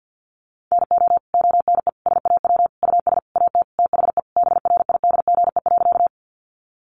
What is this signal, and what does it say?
Morse code, heard as a keyed tone, about 48 wpm, 700 Hz. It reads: NO ONE SAW US AT THE BRIDGE.
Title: Sentences from Top 1200 Words 50wpm